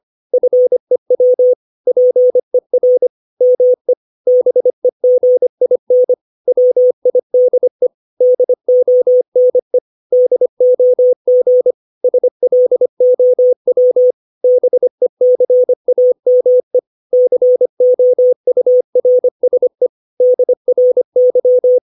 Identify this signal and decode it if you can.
Morse code, heard as a keyed tone, about 25 wpm, 500 Hz. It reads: FEW PER ME BEGIN WIDE DONE DOG SLOW BECAME COURSE DRY
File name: 632 words / 632 Words 25WPM